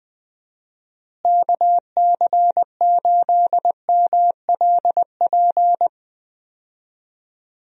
Morse code, heard as a keyed tone, about 20 wpm, 700 Hz. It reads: KC8MLP
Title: USA Call Signs - 6 Characters - Repeated 3x - 20wpm